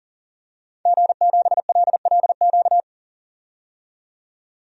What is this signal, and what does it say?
Morse code, heard as a keyed tone, about 40 wpm, 700 Hz. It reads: G7LLQ
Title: DXCC-100 Call Signs - 5 Characters - 40wpm